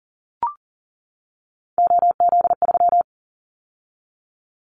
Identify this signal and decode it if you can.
Morse code, heard as a keyed tone, about 40 wpm, 700 Hz. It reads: OZ3
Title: Callsign Prefixes for Top 100 DXCC - 40wpm